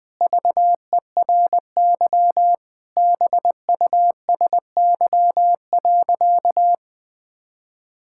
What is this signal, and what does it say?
Morse code, heard as a keyed tone, about 20 wpm, 700 Hz. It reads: VERY BUSY.
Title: Sentences with 7-Word Phrases - Spoken Beginning - 20wpm